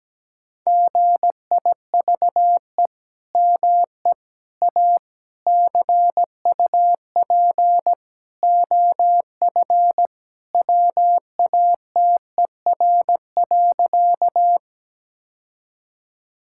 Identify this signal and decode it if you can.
Morse code, heard as a keyed tone, about 17 wpm, 700 Hz. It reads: GIVE ME A CUP OF WATER.
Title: Sentences from Top 1600 Words 17wpm